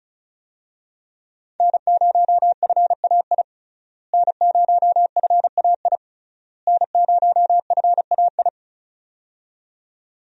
Morse code, heard as a keyed tone, about 36 wpm, 700 Hz. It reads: N0FAI N0FAI N0FAI
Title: USA Call Signs - 5 Characters - Repeated 3x - 35wpm